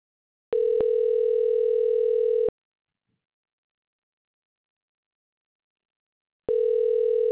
Ringback tone (North America), 440 + 480 Hz.